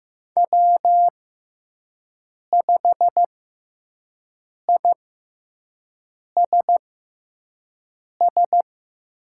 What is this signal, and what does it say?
Morse code, heard as a keyed tone, about 15 wpm, spaced as for 5 wpm, 700 Hz. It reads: W5ISS
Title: USA Call Signs - 5 Characters - 6x Character Spacing - 15wpm